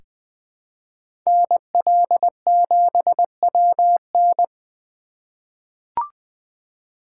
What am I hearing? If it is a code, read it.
Morse code, heard as a keyed tone, about 20 wpm, 700 Hz. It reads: NL7WN
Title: USA Call Signs - 5 Characters - 2x Character Spacing - 20wpm